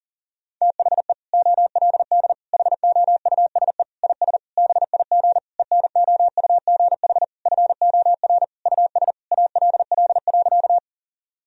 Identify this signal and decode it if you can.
Morse code, heard as a keyed tone, about 37 wpm, 700 Hz. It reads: THE OLD HOUSE IS BIG ENOUGH FOR US ALL.
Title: Sentences from Top 1000 Words - Review Part 1 - 40wpm